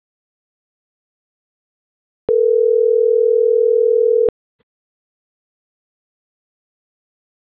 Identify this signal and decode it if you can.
Ringback tone (North America), 440 + 480 Hz.